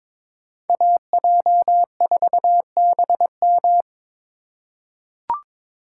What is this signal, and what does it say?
Morse code, heard as a keyed tone, about 22 wpm, 700 Hz. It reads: AJ4BM
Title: USA Call Signs - 5 Characters - 4x Character Spacing - 22wpm